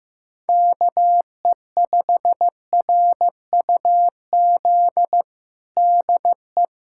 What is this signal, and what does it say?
Morse code, heard as a keyed tone, about 15 wpm, 700 Hz. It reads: KE5RUZ DE
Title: USA Callsign DE Callsign - 6 Characters - 15wpm